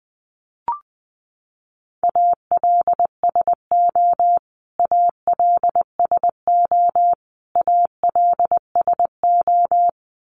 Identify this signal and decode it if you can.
Morse code, heard as a keyed tone, about 20 wpm, 700 Hz. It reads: ALSO ALSO ALSO
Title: Top 300 Words Repeated 20wpm